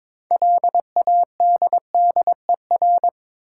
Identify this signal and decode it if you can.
Morse code, heard as a keyed tone, about 22 wpm, 700 Hz. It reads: LADDER
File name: Difficult Words with Hints - 22wpm